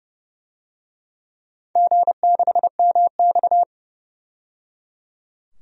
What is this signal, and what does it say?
Morse code, heard as a keyed tone, about 30 wpm, 700 Hz. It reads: G6MX
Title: DXCC-100 Call Signs - 4 Characters - 30wpm